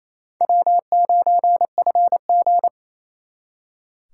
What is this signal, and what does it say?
Morse code, heard as a keyed tone, about 36 wpm, 700 Hz. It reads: W9FG